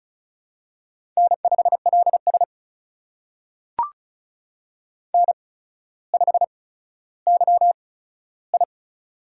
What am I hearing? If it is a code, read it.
Morse code, heard as a keyed tone, about 36 wpm, spaced as for 15 wpm, 700 Hz. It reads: N5LS N5YI
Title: USA Call Signs - 4 Characters - 8x Character Spacing - 35wpm